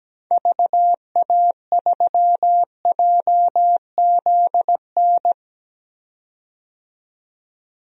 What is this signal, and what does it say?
Morse code, heard as a keyed tone, about 17 wpm, 700 Hz. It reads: VA3JZN
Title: DXCC-100 Call Signs - 6 Characters - 17wpm